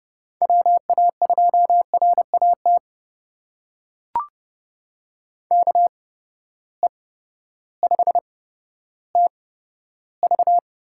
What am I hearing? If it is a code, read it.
Morse code, heard as a keyed tone, about 30 wpm, spaced as for 14 wpm, 700 Hz. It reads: WA2RAT KE5TV
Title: USA Call Signs - 6 Characters - 8x Character Spacing - 30wpm